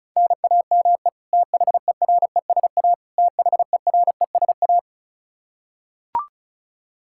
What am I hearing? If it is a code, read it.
Morse code, heard as a keyed tone, about 36 wpm, 700 Hz. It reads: NAME THERESA THERESA
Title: Practice QSO - Exchange 1 - Beginning - 35wpm